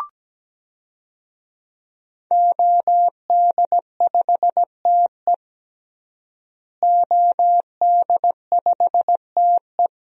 Morse code, heard as a keyed tone, about 17 wpm, 700 Hz. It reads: OD5TE OD5TE
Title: DXCC-100 Call Signs - 5 Characters - Repeated 2x - 17wpm